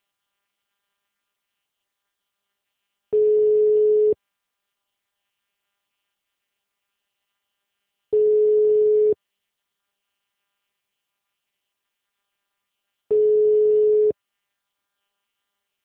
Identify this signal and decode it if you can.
Ringback tone (Europe, 425 Hz).